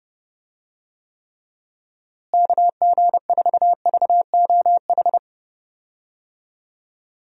Morse code, heard as a keyed tone, about 30 wpm, 700 Hz. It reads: KG4VOH
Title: USA Call Signs - 6 Characters - Repeated 3x - 30wpm